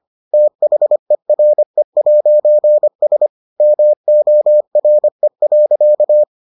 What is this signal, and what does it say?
Morse code, heard as a keyed tone, about 25 wpm, 600 Hz. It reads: THERE'S MORE.